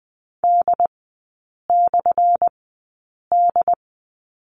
Morse code, heard as a keyed tone, about 20 wpm, spaced as for 8 wpm, 700 Hz. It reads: D/D
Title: 178 Lesson 29e - Slash - Three Characters 20wpm